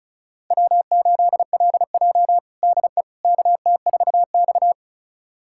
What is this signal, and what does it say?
Morse code, heard as a keyed tone, about 34 wpm, 700 Hz. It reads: W8LJ DE KT4X
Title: USA Callsign DE Callsign - 4 Characters - 35wpm